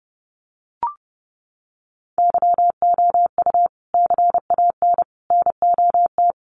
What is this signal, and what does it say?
Morse code, heard as a keyed tone, about 30 wpm, 700 Hz. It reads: YOU CAN NOT
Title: Sets of 5 Words 30wpm